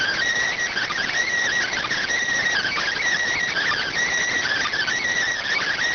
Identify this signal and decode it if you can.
Emergency Alert System (SAME) tones (header bursts), heard as harsh buzzing bursts. It is heard from 0.7 to 4.9 s.